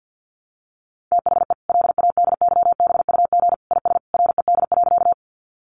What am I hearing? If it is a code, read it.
Morse code, heard as a keyed tone, about 50 wpm, 700 Hz. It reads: THE LADYBUG IS RED.
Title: Additional Easy Phrases 50wpm